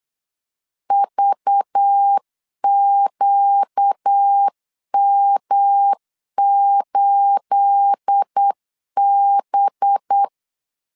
Morse code, heard as a keyed tone, about 8 wpm, 800 Hz. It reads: VQM8B
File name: LEZIONI / 9 28 8wpm 3512